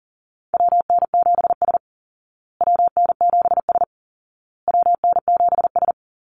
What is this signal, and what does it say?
Morse code, heard as a keyed tone, about 40 wpm, 700 Hz. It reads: WN7S WN7S WN7S
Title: Call Signs Repeated 2x - 40wpm